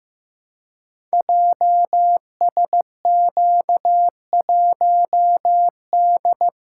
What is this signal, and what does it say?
Morse code, heard as a keyed tone, about 15 wpm, 700 Hz. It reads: JSQ1D
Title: Code Groups - 5 Letter:Numbers - 15wpm